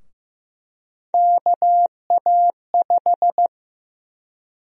Morse code, heard as a keyed tone, about 15 wpm, 700 Hz. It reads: KA5
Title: USA Call Sign Prefixes - 15wpm